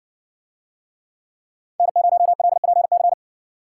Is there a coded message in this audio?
Yes: Morse code from 1.8 to 3.1 s, heard as a keyed tone, about 60 wpm, 700 Hz. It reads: N0LPC